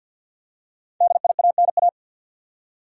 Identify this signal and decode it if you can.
Morse code, heard as a keyed tone, about 49 wpm, 700 Hz. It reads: DIANA
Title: USA Common Names - 50wpm